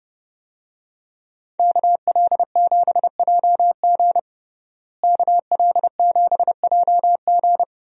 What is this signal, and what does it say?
Morse code, heard as a keyed tone, about 30 wpm, 700 Hz. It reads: KL7JG KL7JG
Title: DXCC-100 Call Signs - 5 Characters - Repeated 2x - 30wpm